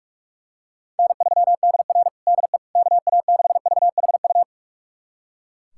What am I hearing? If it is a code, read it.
Morse code, heard as a keyed tone, about 41 wpm, 700 Hz. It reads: N3DR DE KA6VHU
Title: USA Callsign DE Callsign 45wpm